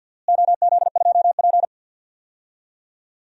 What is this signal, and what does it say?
Morse code, heard as a keyed tone, about 49 wpm, 700 Hz. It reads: OZ2P
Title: DXCC-100 Call Signs - 4 Characters - Repeated 3x - 50wpm